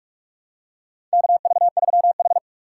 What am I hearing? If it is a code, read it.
Morse code, heard as a keyed tone, about 44 wpm, 690 Hz. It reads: KV3H
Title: USA Call Signs - 4x Character Spacing - 45wpm